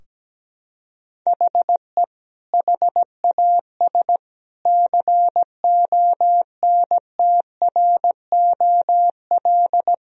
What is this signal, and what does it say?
Morse code, heard as a keyed tone, about 17 wpm, 700 Hz. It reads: HE HAS CONTROL
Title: Sentences from Top 1000 Words - Review Part 3 - 17wpm